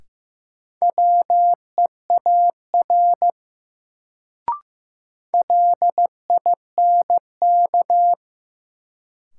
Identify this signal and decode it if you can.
Morse code, heard as a keyed tone, about 15 wpm, 700 Hz. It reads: WEAR LINK
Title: Sentences from Top 1500 Words - New Word Drill Alt - 15wpm